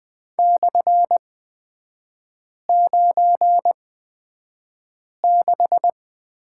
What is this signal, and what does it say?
Morse code, heard as a keyed tone, about 20 wpm, spaced as for 5 wpm, 700 Hz. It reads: /96
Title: 243 Lesson 39e - Letter z - Three Characters 20wpm